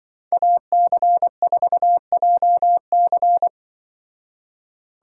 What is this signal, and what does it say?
Morse code, heard as a keyed tone, about 24 wpm, 700 Hz. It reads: AC4JC
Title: Speed Racing - USA Call Signs 20wpm